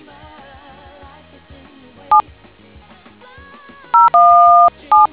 DTMF, heard as telephone touch tones, tones of about 170 ms. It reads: 7 017